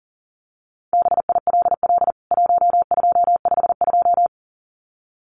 Morse code, heard as a keyed tone, about 40 wpm, 700 Hz. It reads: BILL 1252
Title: CWT Exchange 40wpm